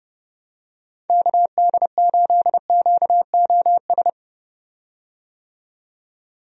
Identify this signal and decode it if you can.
Morse code, heard as a keyed tone, about 32 wpm, 700 Hz. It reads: KD8QOS
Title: USA Call Signs - 6 Characters - 30wpm